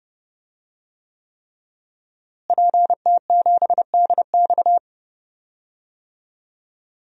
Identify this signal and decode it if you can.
Morse code, heard as a keyed tone, about 30 wpm, 700 Hz. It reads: PT7DX
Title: DXCC-100 Call Signs - 5 Characters - Repeated 3x - 30wpm